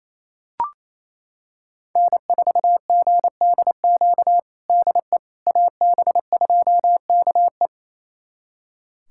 Morse code, heard as a keyed tone, about 28 wpm, 700 Hz. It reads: N4GDQ DE AB2KE
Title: USA Callsign DE Callsign - 5 Characters - 28wpm